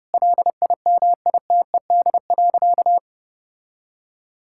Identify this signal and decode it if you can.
Morse code, heard as a keyed tone, about 32 wpm, 700 Hz. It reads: LIMITED.
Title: Sentences from Top 1000 Words - Review Part 6 - 30wpm